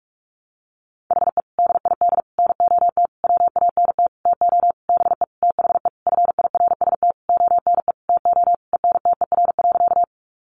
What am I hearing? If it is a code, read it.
Morse code, heard as a keyed tone, about 44 wpm, 700 Hz. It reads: HE DID NOT WANT TO BE THE FIRST ONE TO ENTER.
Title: Sentences from Top 1300 Words 45wpm